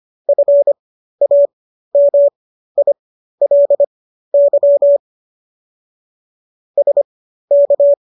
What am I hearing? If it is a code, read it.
Morse code, heard as a keyed tone, about 25 wpm, spaced as for 13 wpm, 570 Hz. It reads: FAMILY SK